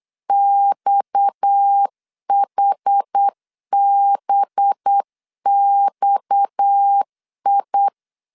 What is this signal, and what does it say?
Morse code, heard as a keyed tone, about 8 wpm, 800 Hz. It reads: XHBXI
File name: LEZIONI / 6 PKX0 / 6 PKX0 8wpm 3341